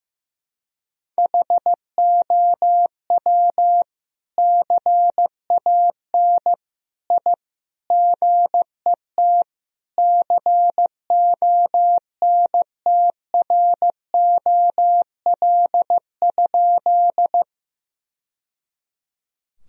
Morse code, heard as a keyed tone, about 15 wpm, 700 Hz. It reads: HOW CAN I GET CONTROL?
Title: Sentences from Top 1000 Words - Review Part 3 - 15wpm